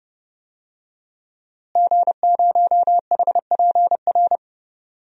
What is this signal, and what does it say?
Morse code, heard as a keyed tone, about 30 wpm, 700 Hz. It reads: G0HPR
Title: DXCC-100 Call Signs - 5 Characters - Repeated 3x - 30wpm